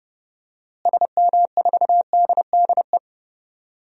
Morse code, heard as a keyed tone, about 30 wpm, 700 Hz. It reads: SM4DDE
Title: DXCC-100 Call Signs - 6 Characters - Repeated 2x - 30wpm